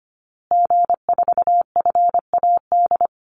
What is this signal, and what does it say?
Morse code, heard as a keyed tone, about 25 wpm, 700 Hz. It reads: G4FAD
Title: Call Signs Repeated 2x - 25wpm